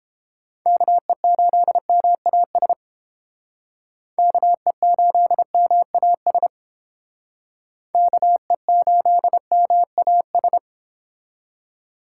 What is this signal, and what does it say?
Morse code, heard as a keyed tone, about 32 wpm, 700 Hz. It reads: KE8MAS KE8MAS KE8MAS